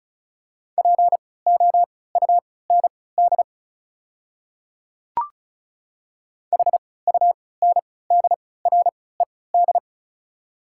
Morse code, heard as a keyed tone, about 34 wpm, spaced as for 20 wpm, 690 Hz. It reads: POUND HUNDRED
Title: 3x Character Spacing - Top 500 Words - 35wpm